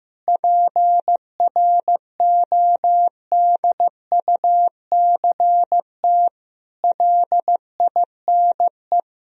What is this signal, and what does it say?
Morse code, heard as a keyed tone, about 16 wpm, 700 Hz. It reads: PRODUCT LINE
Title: Sentences from Top 1000 Words - Review Part 3 - 15wpm